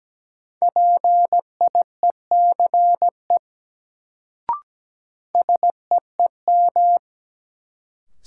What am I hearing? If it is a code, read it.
Morse code, heard as a keyed tone, about 17 wpm, 700 Hz. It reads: PIECE SEEM
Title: Sets of 3 Words - New Word Drill - 17wpm